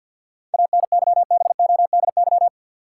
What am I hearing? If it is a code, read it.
Morse code, heard as a keyed tone, about 49 wpm, 700 Hz. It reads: ANYBODY
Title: Sets of 5 Words - New Word Drill - 50wpm